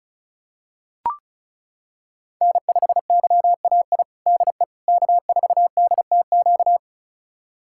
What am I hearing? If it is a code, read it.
Morse code, heard as a keyed tone, about 36 wpm, 700 Hz. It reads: N5YAI DE K4DTQ
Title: USA Callsign DE Callsign 35wpm